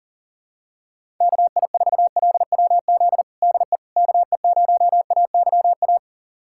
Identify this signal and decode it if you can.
Morse code, heard as a keyed tone, about 40 wpm, 700 Hz. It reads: KI4LWZ DE KE0AYA
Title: USA Callsign DE Callsign - 6 Characters - 40wpm